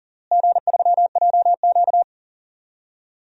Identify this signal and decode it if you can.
Morse code, heard as a keyed tone, about 40 wpm, 700 Hz. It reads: G3JQ
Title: DXCC-100 Call Signs - 4 Characters - Repeated 3x - 40wpm